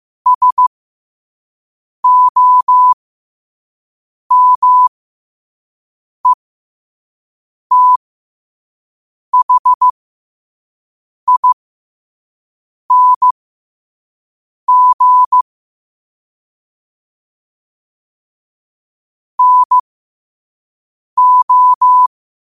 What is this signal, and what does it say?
Morse code, heard as a keyed tone, about 15 wpm, spaced as for 5 wpm, 1000 Hz. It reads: SOMETHING NO
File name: qotdfiles / 5wpm